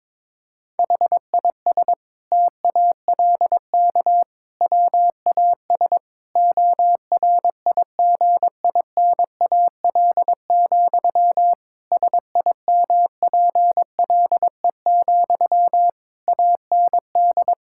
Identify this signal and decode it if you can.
Morse code, heard as a keyed tone, about 22 wpm, 700 Hz. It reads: HIS TALK WAS ORIGINAL, SIMPLE, AND
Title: Sentences from Top 1000 Words - Review Part 6 - 22wpm